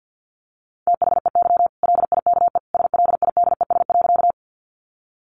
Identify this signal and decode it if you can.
Morse code, heard as a keyed tone, about 50 wpm, 700 Hz. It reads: THEY LIKE SLIDES.
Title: Easy Phrases 50wpm